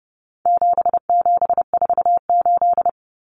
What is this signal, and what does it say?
Morse code, heard as a keyed tone, about 30 wpm, 700 Hz. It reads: 7748
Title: Common Cut Numbers - 4-Digit 30wpm